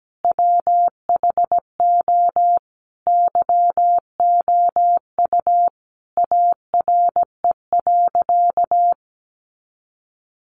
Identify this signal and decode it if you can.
Morse code, heard as a keyed tone, about 17 wpm, 700 Hz. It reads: WHO YOU ARE.
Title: Sentences from Top 700 Words 17wpm